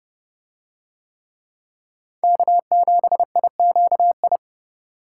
Morse code, heard as a keyed tone, about 29 wpm, 700 Hz. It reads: K7IQI